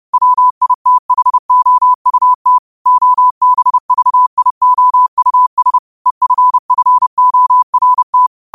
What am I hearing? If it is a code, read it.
Morse code, heard as a keyed tone, about 30 wpm, 1000 Hz. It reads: WITHOUT OBVIOUS EFFORT
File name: qotdfiles / 30wpm